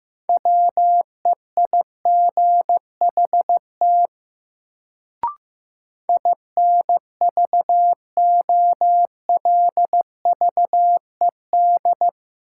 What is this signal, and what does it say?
Morse code, heard as a keyed tone, about 15 wpm, 700 Hz. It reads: WEIGHT INVOLVED
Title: Sentences from Top 800 Words - New Word Drill - 15wpm